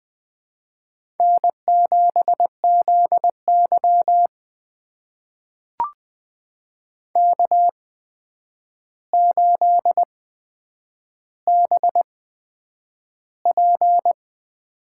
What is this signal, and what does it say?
Morse code, heard as a keyed tone, about 20 wpm, spaced as for 9 wpm, 700 Hz. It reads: N7ZY K8BP
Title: USA Call Signs - 8x Character Spacing - 20wpm